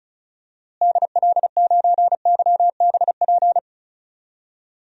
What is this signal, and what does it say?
Morse code, heard as a keyed tone, about 34 wpm, 700 Hz. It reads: DL9YBP
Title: DXCC-100 Call Signs - 6 Characters - Repeated 2x - 35wpm